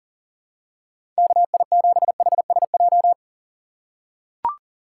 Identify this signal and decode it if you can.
Morse code, heard as a keyed tone, about 40 wpm, 700 Hz. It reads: KI7HSJ
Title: USA Call Signs - 6 Characters - 3x Character Spacing - 40wpm